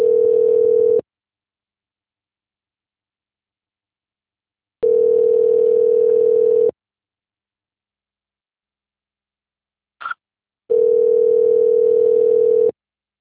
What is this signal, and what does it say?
Ringback tone (North America), 440 + 480 Hz.
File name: out- - - - - .6465